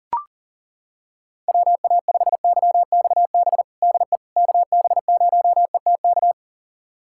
Morse code, heard as a keyed tone, about 37 wpm, 700 Hz. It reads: WA5YXB DE KB0ETK
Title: USA Callsign DE Callsign - 6 Characters - 40wpm